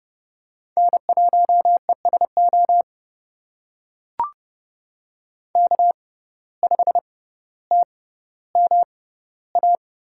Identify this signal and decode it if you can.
Morse code, heard as a keyed tone, about 30 wpm, spaced as for 15 wpm, 700 Hz. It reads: N1ESO K5TMA
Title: USA Call Signs - 5 Characters - 6x Character Spacing - 30wpm